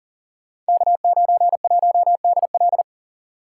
Morse code, heard as a keyed tone, about 38 wpm, 700 Hz. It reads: K91DL
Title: Code Groups - 5 Letter:Numbers - 40wpm